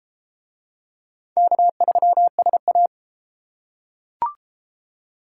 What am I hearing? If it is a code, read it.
Morse code, heard as a keyed tone, about 31 wpm, 700 Hz. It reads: K3SA